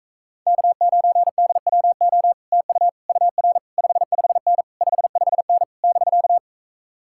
Morse code, heard as a keyed tone, about 44 wpm, 700 Hz. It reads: K9DWO TU UR 55N 55N <BK>